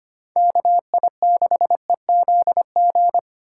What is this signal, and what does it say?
Morse code, heard as a keyed tone, about 25 wpm, 700 Hz. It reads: KI6EZG
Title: USA Call Signs - 6 Characters - 3x Character Spacing - 25wpm